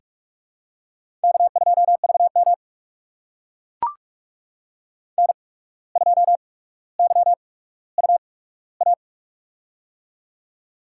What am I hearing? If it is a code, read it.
Morse code, heard as a keyed tone, about 44 wpm, spaced as for 18 wpm, 700 Hz. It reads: K2VM N2YUA
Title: USA Call Signs - 8x Character Spacing - 45wpm